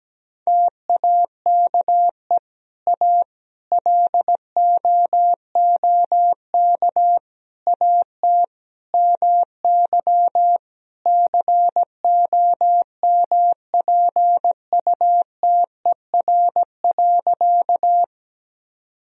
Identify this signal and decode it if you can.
Morse code, heard as a keyed tone, about 17 wpm, 700 Hz. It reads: TAKE A LOOK AT MY COMPUTER.
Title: Sentences from Top 1000 Words - Review Part 3 - 17wpm